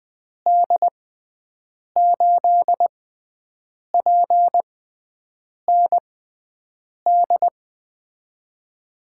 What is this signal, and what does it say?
Morse code, heard as a keyed tone, about 20 wpm, spaced as for 7 wpm, 700 Hz. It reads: D8PND
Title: USA Call Signs - 6 Characters - 6x Character Spacing - 20wpm